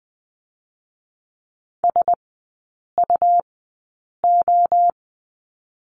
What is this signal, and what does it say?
Morse code, heard as a keyed tone, about 20 wpm, 700 Hz. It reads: S U O
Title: 218 Lesson 35e - Number 0 - Three Characters 20wpm